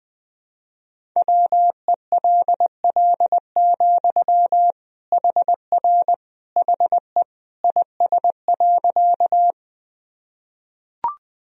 Morse code, heard as a keyed tone, about 21 wpm, 700 Hz. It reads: WELL, HR HE IS.
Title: Sentences from Top 200 Words and Abbreviations 20wpm